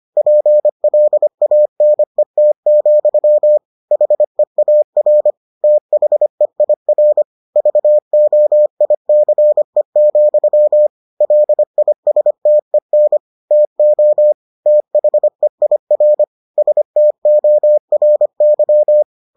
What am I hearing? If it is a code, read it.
Morse code, heard as a keyed tone, about 25 wpm, 600 Hz. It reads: PLANET, HEAR THEIR VOICE, LISTEN TO THEIR STORY